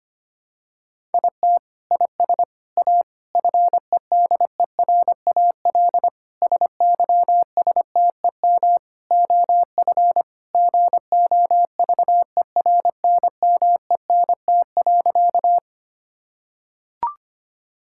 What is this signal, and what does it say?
Morse code, heard as a keyed tone, about 25 wpm, 700 Hz. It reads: IT IS A FEDERAL SYSTEM OF GOVERNMENT.
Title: Sentences from Top 1000 Words - Review Part 5 - 25wpm